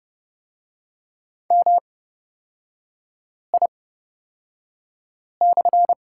Morse code, heard as a keyed tone, about 31 wpm, spaced as for 5 wpm, 690 Hz. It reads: MI/
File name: Lesson 38e - Letter q - Three Characters 30wpm